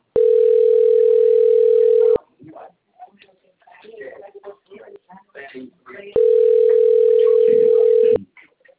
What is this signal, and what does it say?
Ringback tone (North America), 440 + 480 Hz, with speech.
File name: out- - - - - .8776